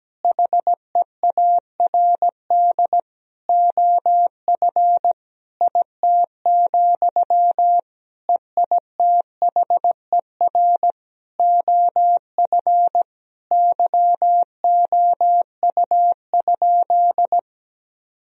Morse code, heard as a keyed tone, about 17 wpm, 700 Hz. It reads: HEARD OF IT, EITHER OF YOU?
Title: Sentences from Top 1000 Words - Review Part 4 - 17wpm